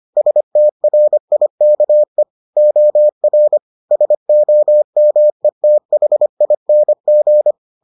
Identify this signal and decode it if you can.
Morse code, heard as a keyed tone, about 25 wpm, 600 Hz. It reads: STRIKE OR SOMETHING